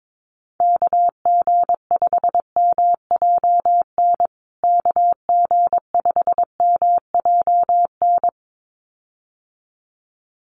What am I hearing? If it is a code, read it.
Morse code, heard as a keyed tone, about 22 wpm, 700 Hz. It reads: KG5MJN KG5MJN
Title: USA Call Signs Repeated 2x 22wpm